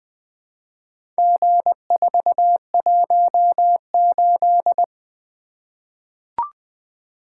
Morse code, heard as a keyed tone, about 20 wpm, 700 Hz. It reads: G418
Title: Four Letter-Number Combinations - 20wpm